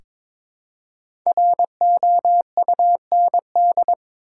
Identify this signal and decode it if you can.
Morse code, heard as a keyed tone, about 23 wpm, 700 Hz. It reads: ROUND
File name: 5 Letter Words - First 3 Letters Given - 22wpm